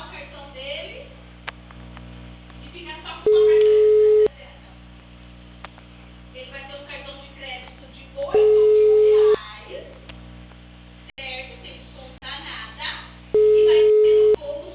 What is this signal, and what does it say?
Ringback tone (Europe, 425 Hz), with speech.
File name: out- -2061- - - .99092